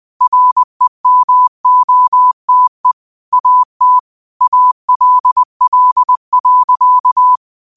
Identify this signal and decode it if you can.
Morse code, heard as a keyed tone, about 20 wpm, 1000 Hz. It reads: REMOTE AT ALL.
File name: 20wpm